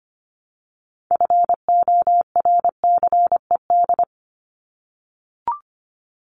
Morse code, heard as a keyed tone, about 25 wpm, 700 Hz. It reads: FORCED
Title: ED Suffix - Top 100 Words with Base Words 25wpm